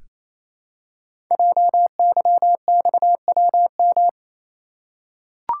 Morse code, heard as a keyed tone, about 27 wpm, 700 Hz. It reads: JYXWM
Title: Code Groups - 5 Letters - 28wpm